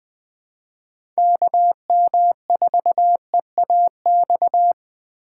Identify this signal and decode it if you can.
Morse code, heard as a keyed tone, about 21 wpm, 700 Hz. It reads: KM4EAX
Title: USA Call Signs - 6 Characters - 4x Character Spacing - 20wpm